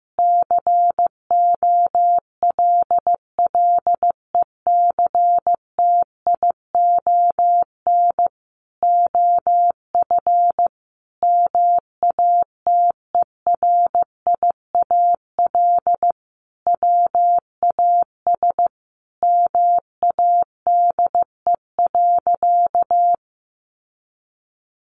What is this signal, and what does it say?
Morse code, heard as a keyed tone, about 15 wpm, 700 Hz. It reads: COLLECTION OF MATERIAL WAS MADE.
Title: Sentences from Top 1200 Words 15wpm